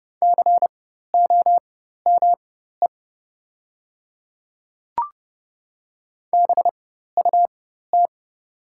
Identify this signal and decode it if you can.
Morse code, heard as a keyed tone, about 31 wpm, spaced as for 14 wpm, 690 Hz. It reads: COME BUT
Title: 4x Character Spacing - Top 100 Words - 30wpm